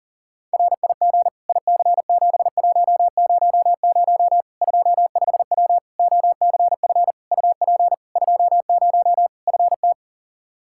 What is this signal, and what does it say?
Morse code, heard as a keyed tone, about 37 wpm, 700 Hz. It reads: RIG IC7100 25W OCF UP 20 FT
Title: K1USN SST QSO Practice 40wpm